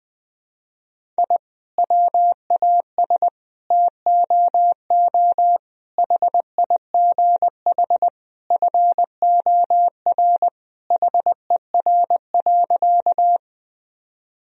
Morse code, heard as a keyed tone, about 20 wpm, 700 Hz. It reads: I WAS TOO HIGH FOR HER.
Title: Sentences from Top 1000 Words - Review Part 1 - 20wpm